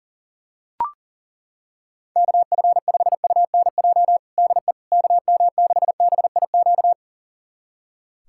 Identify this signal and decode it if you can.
Morse code, heard as a keyed tone, about 40 wpm, 700 Hz. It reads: KF5UNJ DE KM6BIQ
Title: USA Callsign DE Callsign 40wpm